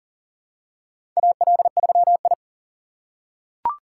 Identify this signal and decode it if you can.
Morse code, heard as a keyed tone, about 40 wpm, 700 Hz. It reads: AL3I
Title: USA Call Signs - 4 Characters - 2x Character Spacing - 40wpm